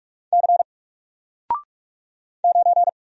Morse code, heard as a keyed tone, about 44 wpm, 700 Hz. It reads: C 9
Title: Single Letter-Number 45wpm